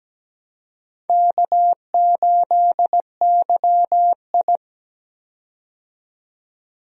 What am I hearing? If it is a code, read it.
Morse code, heard as a keyed tone, about 17 wpm, 700 Hz. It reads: K8YI
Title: USA Call Signs - 4 Characters - 17wpm